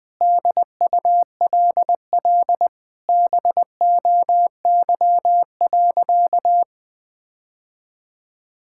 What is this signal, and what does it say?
Morse code, heard as a keyed tone, about 20 wpm, 700 Hz. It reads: DULL BOY.
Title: Sentences with 8-Word Phrases - 20wpm